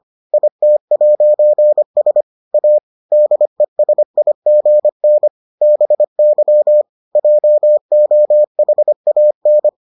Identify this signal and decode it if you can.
Morse code, heard as a keyed tone, about 25 wpm, 600 Hz. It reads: IT'S A DESIGN BY JOHAN